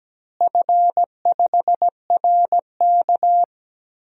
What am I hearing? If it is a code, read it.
Morse code, heard as a keyed tone, about 17 wpm, 700 Hz. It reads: F5RK
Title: DXCC-100 Call Signs - 4 Characters - Repeated 3x - 17wpm